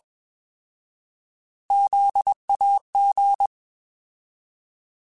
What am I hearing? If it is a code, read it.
Morse code, heard as a keyed tone, about 21 wpm, 800 Hz. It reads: ZAG
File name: Alphabetical / Alphabetical mp3s / Zs @20